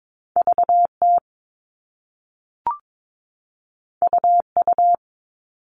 Morse code, heard as a keyed tone, about 22 wpm, 700 Hz. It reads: VT UU
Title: All Cut Numbers - 2-Digit 22wpm